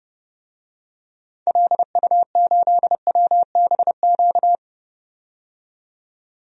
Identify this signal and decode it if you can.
Morse code, heard as a keyed tone, about 32 wpm, 700 Hz. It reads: LU8WBQ
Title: DXCC-100 Call Signs - 6 Characters - 30wpm